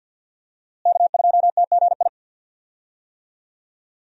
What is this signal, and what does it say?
Morse code, heard as a keyed tone, about 51 wpm, 700 Hz. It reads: K2TGI
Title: USA Call Signs - 5 Characters - Repeated 3x - 50wpm